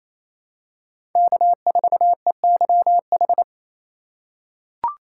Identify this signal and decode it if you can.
Morse code, heard as a keyed tone, about 27 wpm, 700 Hz. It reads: K4EYH
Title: USA Call Signs - 4x Character Spacing - 28wpm